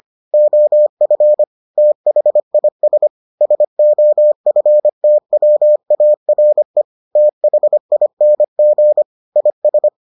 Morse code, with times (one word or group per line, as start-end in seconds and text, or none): 0.3-1.4 OF
1.8-3.1 THIS
3.4-6.8 SOFTWARE
7.2-9.0 THING
9.4-9.9 IS